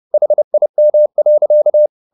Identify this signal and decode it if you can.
Morse code, heard as a keyed tone, about 30 wpm, 600 Hz. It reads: HIM.